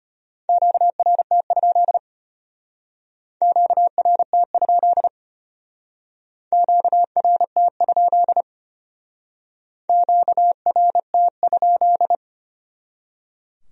Morse code, heard as a keyed tone, about 34 wpm, 700 Hz. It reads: QRT? QRT? QRT? QRT?